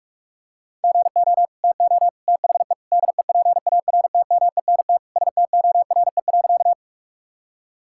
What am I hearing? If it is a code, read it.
Morse code, heard as a keyed tone, about 41 wpm, 700 Hz. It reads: GO TO THE DEPARTMENT STORE.
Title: Sentences from Top 1000 Words - Review Part 5 - 45wpm